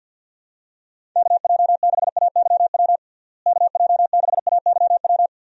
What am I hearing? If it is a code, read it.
Morse code, heard as a keyed tone, about 50 wpm, 700 Hz. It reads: KJ6AYW KJ6AYW
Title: USA Call Signs - 6 Characters - Repeated 3x - 50wpm